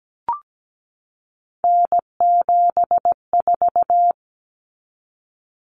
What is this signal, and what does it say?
Morse code, heard as a keyed tone, about 17 wpm, 700 Hz. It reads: N74
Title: Common Cut Numbers - 3-Digit 17wpm